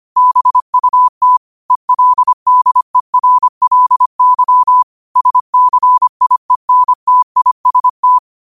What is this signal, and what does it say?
Morse code, heard as a keyed tone, about 25 wpm, 1000 Hz. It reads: BUT ELDERLY SCIENTIST
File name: qotdfiles / 25wpm